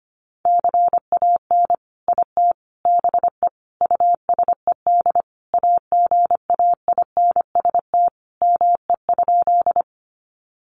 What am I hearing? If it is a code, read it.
Morse code, heard as a keyed tone, about 25 wpm, 700 Hz. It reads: CAN IT BE USED AGAINST ME?
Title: Sentences from Top 600 Words 25wpm